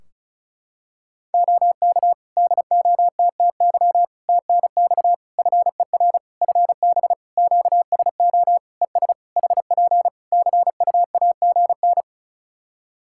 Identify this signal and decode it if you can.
Morse code, heard as a keyed tone, about 36 wpm, 700 Hz. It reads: OK DOTTY TNX FER FB QSO ES HP CUAGN
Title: K1USN SST QSO Practice 35wpm